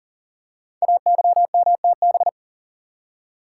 Morse code, heard as a keyed tone, about 40 wpm, 700 Hz. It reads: AYMTB
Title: Code Groups - 5 Letters - 40wpm